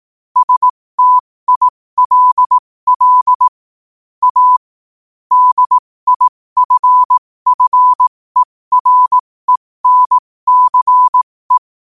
Morse code, heard as a keyed tone, about 18 wpm, spaced as for 15 wpm, 1000 Hz. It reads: STILL A DIFFERENCE
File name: qotdfiles / 15wpm